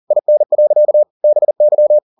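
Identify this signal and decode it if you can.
Morse code, heard as a keyed tone, about 40 wpm, 600 Hz. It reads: IN. BY
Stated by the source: Morse code at 40 wpm, practice text, CW med Gnister.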